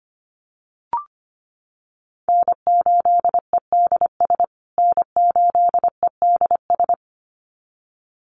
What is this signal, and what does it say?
Morse code, heard as a keyed tone, about 25 wpm, 700 Hz. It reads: N8EDS N8EDS
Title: USA Call Signs Repeated 2x 25wpm